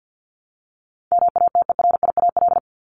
Morse code, heard as a keyed tone, about 48 wpm, 700 Hz. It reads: MATERIAL
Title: Top 751-1000 Words 50wpm